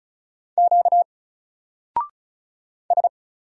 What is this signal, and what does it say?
Morse code, heard as a keyed tone, about 36 wpm, 700 Hz. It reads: Q S
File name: Single-Letter-Number-35wpm